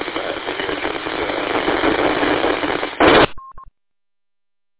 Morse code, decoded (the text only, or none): none